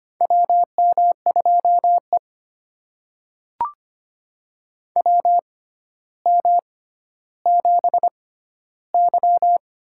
Morse code, heard as a keyed tone, about 25 wpm, spaced as for 13 wpm, 700 Hz. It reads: WM2E WM7Y
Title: USA Call Signs - 4 Characters - 6x Character Spacing - 25wpm